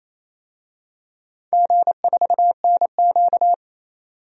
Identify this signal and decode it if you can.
Morse code, heard as a keyed tone, about 28 wpm, 700 Hz. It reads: G4NQ
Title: DXCC-100 Call Signs - 4 Characters - 28wpm